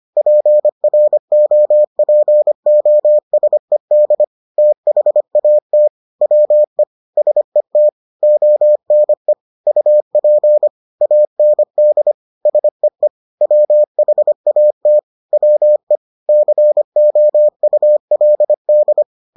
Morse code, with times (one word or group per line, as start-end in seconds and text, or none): 0.2-4.2 PROPOSED
4.6-5.9 THAT
6.2-6.8 WE
7.2-7.9 SET
8.2-9.3 ONE
9.7-10.7 UP
11.0-12.1 AND
12.5-13.1 SEE
13.4-15.0 WHAT
15.3-16.0 WE
16.3-19.0 COULD